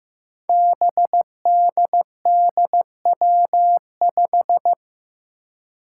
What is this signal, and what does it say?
Morse code, heard as a keyed tone, about 16 wpm, 700 Hz. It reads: BDDW5
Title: Five Challenging Letter-Number Combinations 15wpm